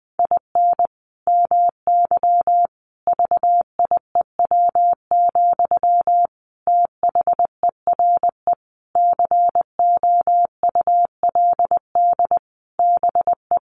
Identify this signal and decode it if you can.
Morse code, heard as a keyed tone, about 19 wpm, 700 Hz. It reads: IN MY VIEW, THERE COULD BE
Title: Sentences from Top 1500 Words 20wpm